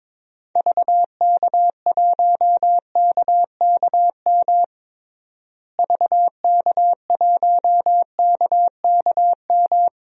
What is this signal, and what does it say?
Morse code, heard as a keyed tone, about 22 wpm, 700 Hz. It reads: VK1KKM VK1KKM
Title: DXCC-100 Call Signs - 6 Characters - Repeated 3x - 22wpm